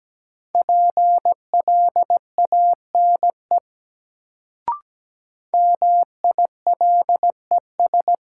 Morse code, heard as a keyed tone, about 17 wpm, 700 Hz. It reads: PLANE MILES
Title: Sentences from Top 1000 Words - Review Part 3 - New Word Drill - 17wpm